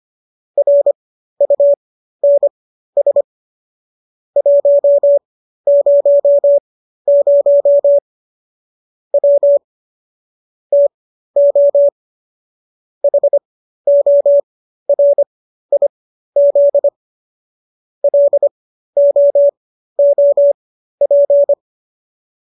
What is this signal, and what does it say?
Morse code, heard as a keyed tone, about 25 wpm, spaced as for 13 wpm, 570 Hz. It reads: RUNS 100 W TO HORIZ LOOP